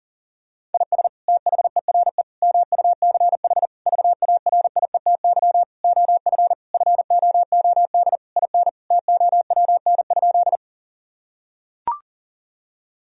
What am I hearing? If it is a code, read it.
Morse code, heard as a keyed tone, about 37 wpm, 700 Hz. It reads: IS THERE MUCH VARIETY OF FOOD IN TOWN?
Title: Sentences from Top 1000 Words - Review Part 4 - 40wpm